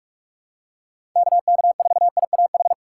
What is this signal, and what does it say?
Morse code, heard as a keyed tone, about 41 wpm, 700 Hz. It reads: KK4IAH
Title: USA Call Signs - 6 Characters - 4x Character Spacing - 45wpm